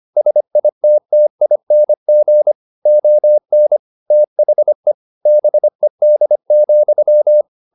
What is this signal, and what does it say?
Morse code, heard as a keyed tone, about 25 wpm, 600 Hz. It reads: SITTING ON THE BED,
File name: the old man and the sea part 1 25wpm